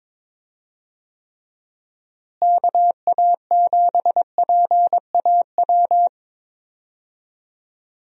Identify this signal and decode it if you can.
Morse code, heard as a keyed tone, about 22 wpm, 700 Hz. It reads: KA7PAW
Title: USA Call Signs - 6 Characters - Repeated 3x - 22wpm